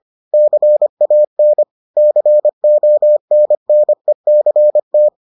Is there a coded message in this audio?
Yes: Morse code from 0.3 to 5.1 s, heard as a keyed tone, about 25 wpm, 600 Hz. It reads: CAN CONNECT